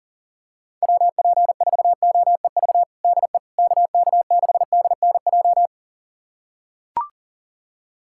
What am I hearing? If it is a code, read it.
Morse code, heard as a keyed tone, about 37 wpm, 700 Hz. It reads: WP4OEV DE KK6DNJ
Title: USA Callsign DE Callsign - 6 Characters - 40wpm